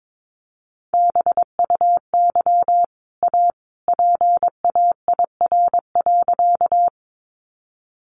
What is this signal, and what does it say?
Morse code, heard as a keyed tone, about 22 wpm, 700 Hz. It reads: BUY A PAIR.
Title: Sentences from Top 1200 Words 22wpm